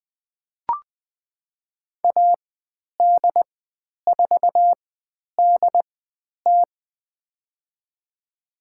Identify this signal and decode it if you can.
Morse code, heard as a keyed tone, about 20 wpm, spaced as for 10 wpm, 700 Hz. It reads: AD4DT
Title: 109 Lesson 18g - Letter m - Call Signs 20wpm